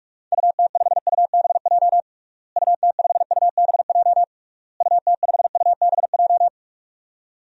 Morse code, heard as a keyed tone, about 42 wpm, 700 Hz. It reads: UT5UBJ UT5UBJ UT5UBJ
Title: DXCC-100 Call Signs - 6 Characters - Repeated 3x - 45wpm